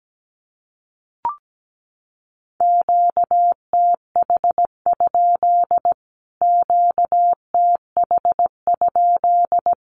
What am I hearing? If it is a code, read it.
Morse code, heard as a keyed tone, about 17 wpm, 700 Hz. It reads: QTH? QTH?
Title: Q-Codes Repeated 3x 17wpm